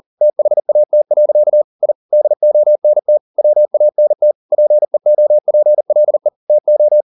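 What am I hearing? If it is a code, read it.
Morse code, heard as a keyed tone, about 40 wpm, 600 Hz. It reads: THAT. I DONT WANT PEOPLE TO
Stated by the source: Morse code at 40 wpm, practice text, CW med Gnister.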